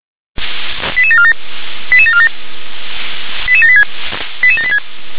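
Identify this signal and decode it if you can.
Five-tone selective call (ZVEI), heard as quick runs of five tones, tones of about 70 ms. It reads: 88747 88747 88786 88786
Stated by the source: ZVEI five-tone selective call 88915, received by SDR.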